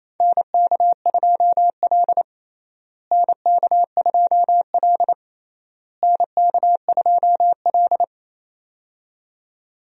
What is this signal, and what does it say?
Morse code, heard as a keyed tone, about 28 wpm, 700 Hz. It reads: NK2L NK2L NK2L
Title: USA Call Signs - 4 Characters - Repeated 3x - 28wpm